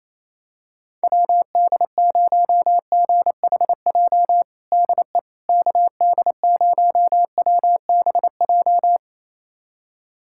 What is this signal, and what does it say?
Morse code, heard as a keyed tone, about 28 wpm, 700 Hz. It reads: WD0GHJ DE KD0WBJ
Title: USA Callsign DE Callsign - 6 Characters - 28wpm